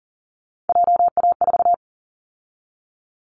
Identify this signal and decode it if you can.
Morse code, heard as a keyed tone, about 37 wpm, 700 Hz. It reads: JA4
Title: Callsign Prefixes for Top 25 DXCC - 40wpm